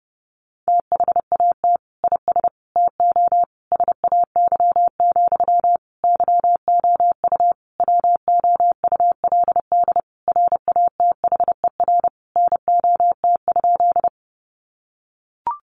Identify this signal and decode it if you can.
Morse code, heard as a keyed tone, about 30 wpm, 700 Hz. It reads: THAT IS TO SAY, YOU WOULD RATHER NOT?
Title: Sentences from Top 700 Words 30wpm